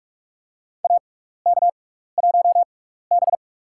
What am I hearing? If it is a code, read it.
Morse code, heard as a keyed tone, about 46 wpm, spaced as for 16 wpm, 700 Hz. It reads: AK1B
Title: USA Call Signs - 4 Characters - 6x Character Spacing - 45wpm